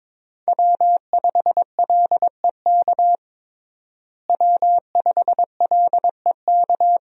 Morse code, heard as a keyed tone, about 23 wpm, 700 Hz. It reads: W5LEK W5LEK
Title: USA Call Signs - 5 Characters - Repeated 3x - 22wpm